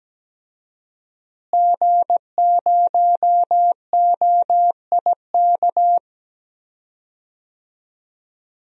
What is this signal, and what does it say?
Morse code, heard as a keyed tone, about 17 wpm, 700 Hz. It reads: G0OIK
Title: DXCC-100 Call Signs - 5 Characters - Repeated 2x - 17wpm